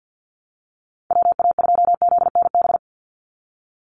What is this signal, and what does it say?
Morse code, heard as a keyed tone, about 50 wpm, 700 Hz. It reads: WA2ZNB
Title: USA Call Signs 50wpm